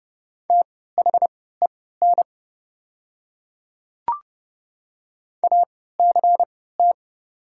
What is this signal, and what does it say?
Morse code, heard as a keyed tone, about 30 wpm, spaced as for 17 wpm, 690 Hz. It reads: THEN ACT
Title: 3x Character Spacing - Top 300 Words - 30wpm